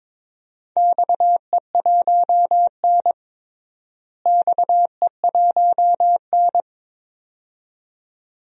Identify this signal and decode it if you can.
Morse code, heard as a keyed tone, about 22 wpm, 700 Hz. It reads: XE1N XE1N
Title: DXCC-100 Call Signs - 4 Characters - Repeated 2x - 22wpm